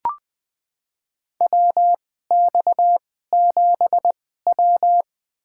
Morse code, heard as a keyed tone, about 20 wpm, spaced as for 15 wpm, 700 Hz. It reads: WX7W
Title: USA Call Signs - 4 Characters - 2x Character Spacing - 20wpm